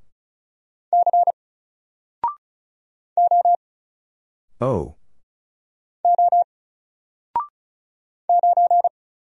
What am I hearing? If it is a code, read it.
Morse code, heard as a keyed tone, about 34 wpm, 700 Hz. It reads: C O O 9
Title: Single Character (Letter:Number:Punctuation:Prosign) Alt 35wpm